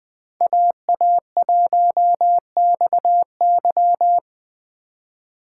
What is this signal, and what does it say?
Morse code, heard as a keyed tone, about 19 wpm, 700 Hz. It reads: AA1XY